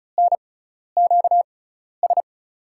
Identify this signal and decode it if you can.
Morse code, heard as a keyed tone, about 36 wpm, spaced as for 12 wpm, 700 Hz. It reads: NQS
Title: USA Call Signs - 6 Characters - 6x Character Spacing - 35wpm